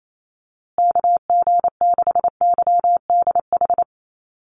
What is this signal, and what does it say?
Morse code, heard as a keyed tone, about 28 wpm, 700 Hz. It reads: KG6YDH
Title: USA Call Signs Repeated 3x 28wpm